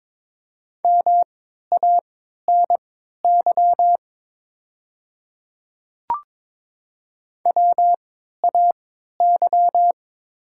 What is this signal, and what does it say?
Morse code, heard as a keyed tone, about 22 wpm, spaced as for 13 wpm, 700 Hz. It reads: MANY WAY
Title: 3x Character Spacing - Top 100 Words - 22wpm